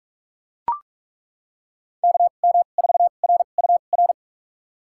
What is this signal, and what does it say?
Morse code, heard as a keyed tone, about 41 wpm, spaced as for 31 wpm, 700 Hz. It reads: KM4RUR
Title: USA Call Signs - 6 Characters - 2x Character Spacing - 45wpm